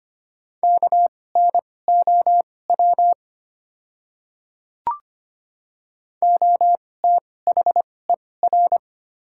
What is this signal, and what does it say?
Morse code, heard as a keyed tone, about 25 wpm, spaced as for 18 wpm, 700 Hz. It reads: KNOW OTHER
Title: 2x Character Spacing - Top 300 Words - 25wpm